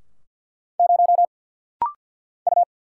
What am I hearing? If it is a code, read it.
Morse code, heard as a keyed tone, about 50 wpm, 700 Hz. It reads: 0 U
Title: Single Letter-Number - Rapid-Fire 50wpm